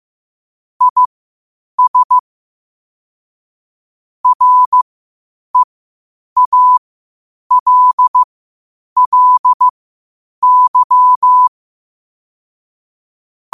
Morse code, heard as a keyed tone, about 15 wpm, spaced as for 8 wpm, 1000 Hz. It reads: IS REALLY
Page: Quotes for Fri, 15 Aug 2025 in Morse Code at 8 words per minute.